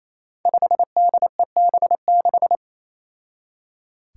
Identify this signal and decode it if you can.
Morse code, heard as a keyed tone, about 28 wpm, 700 Hz. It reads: 5DEB6
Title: Five Challenging Letter-Number Combinations 28wpm